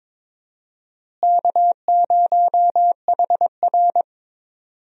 Morse code, heard as a keyed tone, about 22 wpm, 700 Hz. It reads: K0HR
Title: USA Call Signs - 4 Characters - Repeated 3x - 22wpm